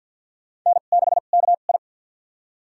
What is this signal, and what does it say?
Morse code, heard as a keyed tone, about 47 wpm, spaced as for 35 wpm, 700 Hz. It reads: N6KI
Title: USA Call Signs - 4 Characters - 2x Character Spacing - 50wpm